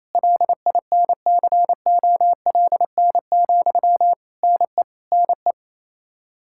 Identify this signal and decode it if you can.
Morse code, heard as a keyed tone, about 27 wpm, 700 Hz. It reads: LINCOLN, NE NE
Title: Practice QSO - Exchange 1 - Beginning - 28wpm